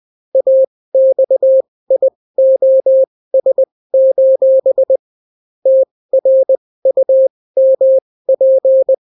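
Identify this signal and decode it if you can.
Morse code, heard as a keyed tone, about 20 wpm, spaced as for 16 wpm, 530 Hz. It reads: AXIOS: TRUMP
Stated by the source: Daily Morse weather and top stories at 16 wpm, clean without QRN.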